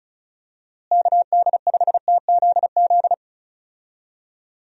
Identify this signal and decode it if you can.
Morse code, heard as a keyed tone, about 34 wpm, 700 Hz. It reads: KD5TZZ
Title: USA Call Signs - 6 Characters - Repeated 2x - 35wpm